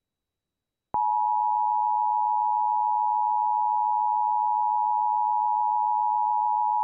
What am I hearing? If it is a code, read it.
Emergency Alert System (SAME) tones (the 853 and 960 Hz attention tone). It is heard from 1.1 to 6.2 s.